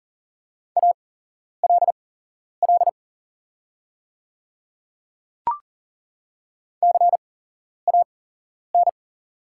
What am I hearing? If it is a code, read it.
Morse code, heard as a keyed tone, about 40 wpm, spaced as for 11 wpm, 690 Hz. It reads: ALL CAN
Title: 8x Character Spacing - Top 200 Words - 40wpm